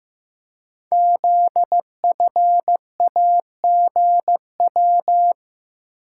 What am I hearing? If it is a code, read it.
Morse code, heard as a keyed tone, about 15 wpm, 700 Hz. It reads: ZFAGW
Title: Code Groups - 5 Letters - 15wpm